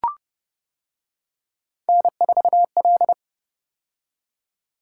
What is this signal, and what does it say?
Morse code, heard as a keyed tone, about 28 wpm, 700 Hz. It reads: N4L